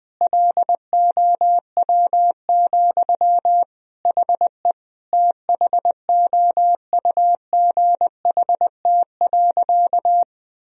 Morse code, heard as a keyed tone, about 21 wpm, 700 Hz. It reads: LOW, HE THOUGHT.
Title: Sentences from Top 1000 Words - Review Part 1 - 20wpm